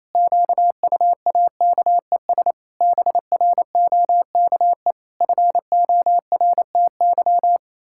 Morse code, heard as a keyed tone, about 27 wpm, 700 Hz. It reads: QUAKES BROKE FORTY
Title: Pangram Challenge v2 - 28wpm